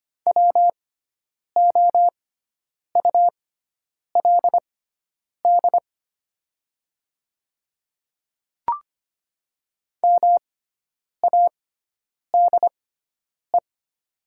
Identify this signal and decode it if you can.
Morse code, heard as a keyed tone, about 25 wpm, spaced as for 9 wpm, 700 Hz. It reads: WOULD MADE
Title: 6x Character Spacing - Top 200 Words - 25wpm